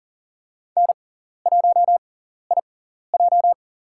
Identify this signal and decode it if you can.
Morse code, heard as a keyed tone, about 38 wpm, spaced as for 14 wpm, 700 Hz. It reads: N1IJ
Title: USA Call Signs - 4 Characters - 6x Character Spacing - 40wpm